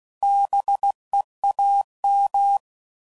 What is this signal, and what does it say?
Morse code, heard as a keyed tone, about 16 wpm, 800 Hz. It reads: BEAM
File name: CW Code Groups @15WPM / QSOs / WD0FBT